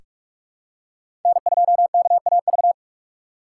Morse code, heard as a keyed tone, about 46 wpm, 700 Hz. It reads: N2KAV
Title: USA Call Signs - 5 Characters - 3x Character Spacing - 45wpm